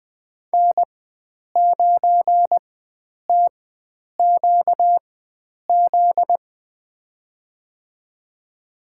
Morse code, heard as a keyed tone, about 20 wpm, spaced as for 9 wpm, 700 Hz. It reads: N9TQZ
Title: USA Call Signs - 5 Characters - 4x Character Spacing - 20wpm